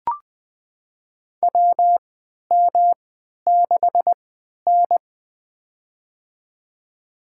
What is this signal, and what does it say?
Morse code, heard as a keyed tone, about 21 wpm, spaced as for 11 wpm, 700 Hz. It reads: WM6N
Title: USA Call Signs - 4 Characters - 3x Character Spacing - 20wpm